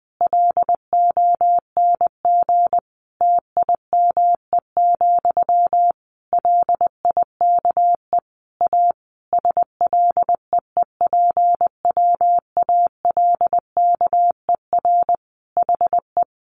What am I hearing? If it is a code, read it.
Morse code, heard as a keyed tone, about 19 wpm, 700 Hz. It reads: LONG TIME, LIKE A SLEEPWALKER HE
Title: A Sound Of Thunder 20wpm